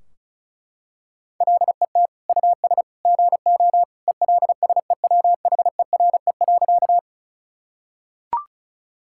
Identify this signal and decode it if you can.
Morse code, heard as a keyed tone, about 36 wpm, 700 Hz. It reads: LET US GO ELSEWHERE.
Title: Sentences from Top 1600 Words 35wpm